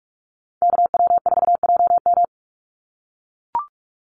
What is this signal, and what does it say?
Morse code, heard as a keyed tone, about 44 wpm, 700 Hz. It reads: KW4JM
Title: NA SOTA Activator Callsigns 45wpm